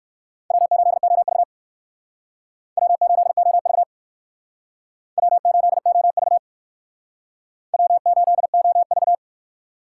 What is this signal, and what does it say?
Morse code, heard as a keyed tone, about 59 wpm, 700 Hz. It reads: W8OV W8OV W8OV W8OV